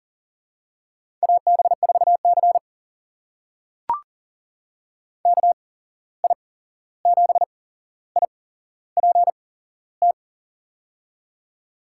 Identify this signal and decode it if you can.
Morse code, heard as a keyed tone, about 40 wpm, spaced as for 15 wpm, 700 Hz. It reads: AB4C KI7IPT
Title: USA Call Signs - 8x Character Spacing - 40wpm